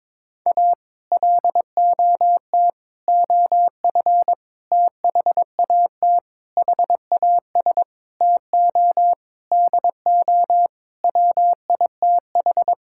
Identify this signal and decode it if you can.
Morse code, heard as a keyed tone, about 21 wpm, 700 Hz. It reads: A LOT OF THAT HAS TO DO WITH
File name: practice / 22wpm / Sets of 8 Words - 22wpm